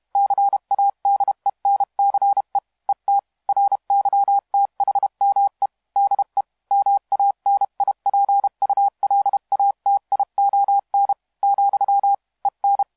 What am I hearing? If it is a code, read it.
Morse code, heard as a keyed tone, about 32 wpm, 800 Hz. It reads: CADENCE ET RYTHME DE MANIPULATION , EN
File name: musiqueradio / cw hsc demo f6ddr